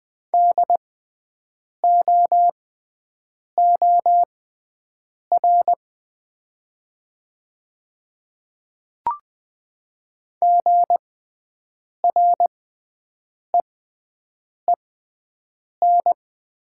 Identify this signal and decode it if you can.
Morse code, heard as a keyed tone, about 20 wpm, spaced as for 7 wpm, 700 Hz. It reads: DOOR GREEN
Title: 6x Character Spacing - Top 500 Words - 20wpm